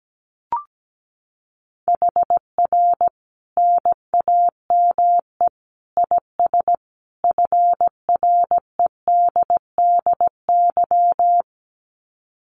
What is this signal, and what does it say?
Morse code, heard as a keyed tone, about 17 wpm, 700 Hz. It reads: HR NAME IS FREDDY
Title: Common QSO Phrases 17wpm